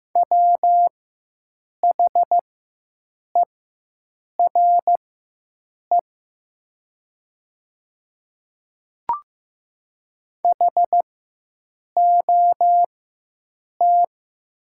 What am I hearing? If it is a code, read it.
Morse code, heard as a keyed tone, about 15 wpm, spaced as for 7 wpm, 700 Hz. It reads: WHERE HOT
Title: 4x Character Spacing - Top 200 Words - 15wpm